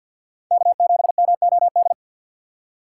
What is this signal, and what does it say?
Morse code, heard as a keyed tone, about 47 wpm, 690 Hz. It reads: K7MOD
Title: USA Call Signs - 5 Characters - 4x Character Spacing - 50wpm